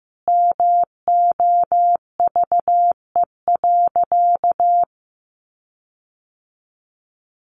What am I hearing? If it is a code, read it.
Morse code, heard as a keyed tone, about 15 wpm, 700 Hz. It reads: MOVE.
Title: Sentences from Top 1300 Words 15wpm